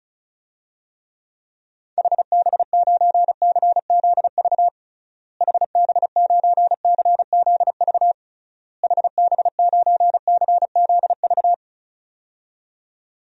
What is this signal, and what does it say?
Morse code, heard as a keyed tone, about 36 wpm, 700 Hz. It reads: HB9CZV HB9CZV HB9CZV
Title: DXCC-100 Call Signs - 6 Characters - Repeated 3x - 35wpm